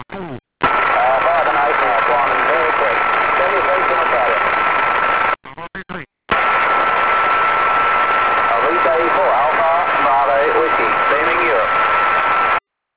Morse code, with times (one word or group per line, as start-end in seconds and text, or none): none